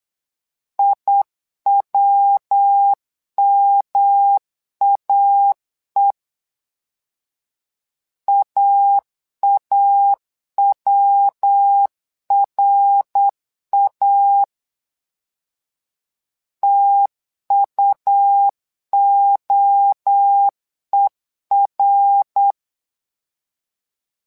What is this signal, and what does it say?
Morse code, heard as a keyed tone, about 8 wpm, 800 Hz. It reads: IWMAE AAWRA TUOER
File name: LEZIONI / 3 MWRO / 3 MWRO 8wpm 2606